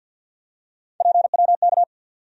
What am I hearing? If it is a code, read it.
Morse code, heard as a keyed tone, about 47 wpm, 700 Hz. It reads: PWK
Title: USA Call Sign Postfixes - 3 Letters 50wpm